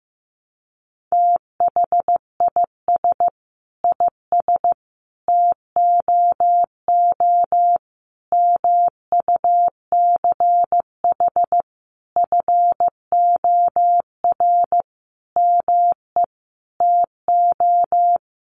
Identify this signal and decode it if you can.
Morse code, heard as a keyed tone, about 15 wpm, 700 Hz. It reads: THIS IS TOO MUCH FOR ME TO
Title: A Sound Of Thunder 15wpm